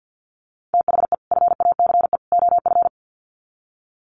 Morse code, heard as a keyed tone, about 50 wpm, 700 Hz. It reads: THE FACE OF
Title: Sets of 3 Words 50wpm